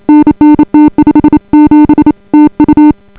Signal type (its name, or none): Morse code